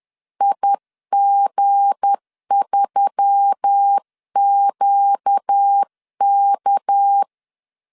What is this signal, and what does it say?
Morse code, heard as a keyed tone, about 10 wpm, 800 Hz. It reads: IG3QK
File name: LEZIONI / 7 GF9Z / 7 GF 10wpm 2514